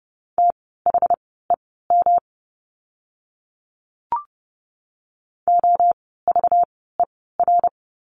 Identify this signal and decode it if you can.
Morse code, heard as a keyed tone, about 30 wpm, spaced as for 17 wpm, 700 Hz. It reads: THEM OVER
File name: course / 30:10wpm / 230 Lesson 37f - Letter x - Words 30wpm